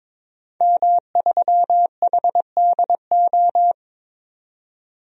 Morse code, heard as a keyed tone, about 22 wpm, 700 Hz. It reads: M3HDO
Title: DXCC-100 Call Signs - 5 Characters - 22wpm